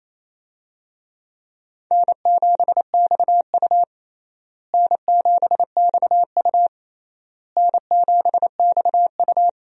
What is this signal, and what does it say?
Morse code, heard as a keyed tone, about 28 wpm, 700 Hz. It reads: N7XU N7XU N7XU
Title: USA Call Signs - 4 Characters - Repeated 3x - 28wpm